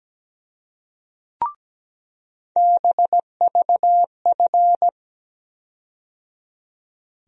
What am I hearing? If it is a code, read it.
Morse code, heard as a keyed tone, about 17 wpm, 700 Hz. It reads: BVF
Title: USA Call Sign Postfixes - 3 Letters 17wpm